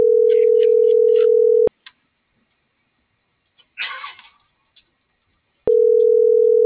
Ringback tone (North America), 440 + 480 Hz.
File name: CENTRAL MABE / out cristopher.carrera